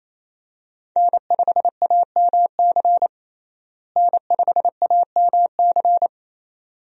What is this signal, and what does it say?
Morse code, heard as a keyed tone, about 27 wpm, 700 Hz. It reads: N5AMC N5AMC
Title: USA Call Signs - 5 Characters - Repeated 2x - 28wpm